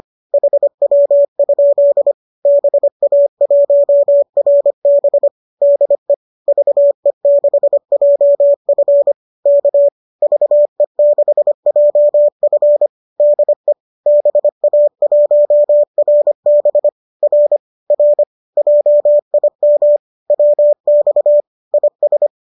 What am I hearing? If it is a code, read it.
Morse code, heard as a keyed tone, about 25 wpm, 590 Hz. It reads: HW? BA1RB DE VE6JF K VE6JF DE BA1RB R R JIM WX IS
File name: qso203 25